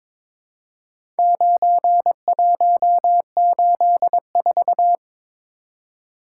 Morse code, heard as a keyed tone, about 22 wpm, 700 Hz. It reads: 9184
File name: practice / 22wpm / Four Letter-Number Combinations - 22wpm